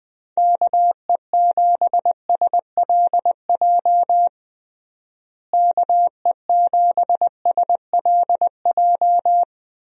Morse code, heard as a keyed tone, about 21 wpm, 700 Hz. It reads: KE7SLJ KE7SLJ
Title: USA Call Signs - 6 Characters - Repeated 3x - 20wpm